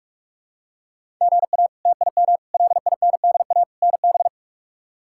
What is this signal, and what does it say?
Morse code, heard as a keyed tone, about 46 wpm, 700 Hz. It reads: GA TIM LINDA NB
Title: K1USN SST QSO Practice - 45wpm